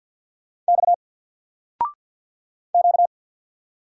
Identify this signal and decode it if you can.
Morse code, heard as a keyed tone, about 48 wpm, 700 Hz. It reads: X Q
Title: Single Character (Letter:Number:Punctuation:Prosign) 50wpm